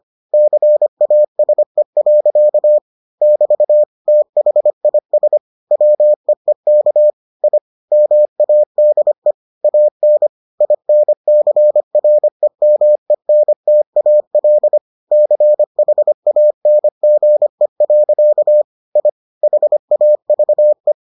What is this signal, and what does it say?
Morse code, heard as a keyed tone, about 25 wpm, 600 Hz. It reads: CASE. <BT> THIS WEEK I MADE AN INCREMENTAL CHANGE. I HAVE